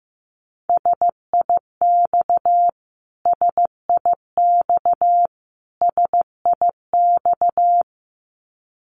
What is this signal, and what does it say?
Morse code, heard as a keyed tone, about 15 wpm, 700 Hz. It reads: SIX SIX SIX
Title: Top 500 Words Repeated 15wpm